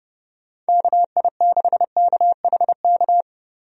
Morse code, heard as a keyed tone, about 30 wpm, 700 Hz. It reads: KI6KHK
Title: USA Call Signs - 6 Characters - 30wpm